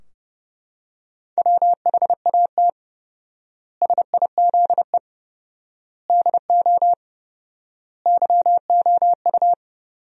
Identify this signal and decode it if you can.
Morse code, heard as a keyed tone, about 31 wpm, 700 Hz. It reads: WHAT SIZE DO YOU
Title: 4x Word-Spacing - Sentences from Top 400 Words - 30wpm